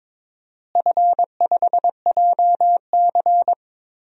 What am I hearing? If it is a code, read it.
Morse code, heard as a keyed tone, about 22 wpm, 700 Hz. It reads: F5JC
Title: DXCC-100 Call Signs - 4 Characters - Repeated 3x - 22wpm